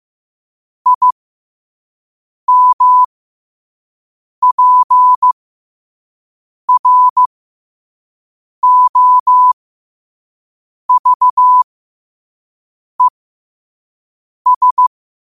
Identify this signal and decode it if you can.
Morse code, heard as a keyed tone, about 15 wpm, spaced as for 5 wpm, 1000 Hz. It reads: IMPROVES
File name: qotdfiles / 5wpm